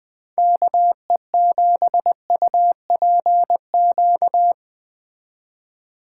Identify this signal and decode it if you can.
Morse code, heard as a keyed tone, about 20 wpm, 700 Hz. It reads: KE7UPQ